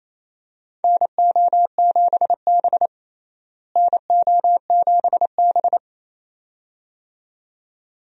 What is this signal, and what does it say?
Morse code, heard as a keyed tone, about 28 wpm, 700 Hz. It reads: NO7B NO7B
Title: USA Call Signs - 4 Characters - Repeated 2x - 28wpm